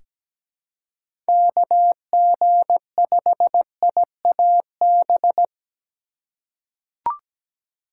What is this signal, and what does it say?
Morse code, heard as a keyed tone, about 17 wpm, 700 Hz. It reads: KG5IAB
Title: USA Call Signs - 6 Characters - 4x Character Spacing - 17wpm